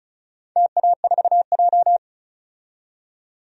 Morse code, heard as a keyed tone, about 34 wpm, 700 Hz. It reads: TA4J
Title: DXCC-100 Call Signs - 4 Characters - 35wpm